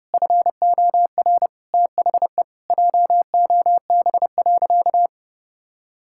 Morse code, heard as a keyed tone, about 30 wpm, 700 Hz. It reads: FOR THE JOB.
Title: Sentences from Top 1600 Words 30wpm